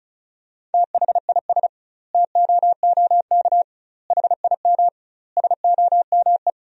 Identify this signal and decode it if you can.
Morse code, heard as a keyed tone, about 36 wpm, 700 Hz. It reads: THIS TOOK HIM SOME
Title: 2x Word-Spacing - Sentences from Top 300 Words - 35wpm